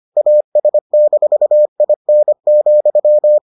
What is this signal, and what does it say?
Morse code, heard as a keyed tone, about 25 wpm, 600 Hz. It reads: AS-IN,